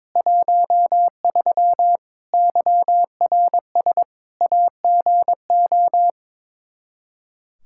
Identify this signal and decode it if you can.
Morse code, heard as a keyed tone, about 22 wpm, 700 Hz. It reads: 13 YRS AGO
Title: Practice QSO - Exchange 3 - More Info - 22wpm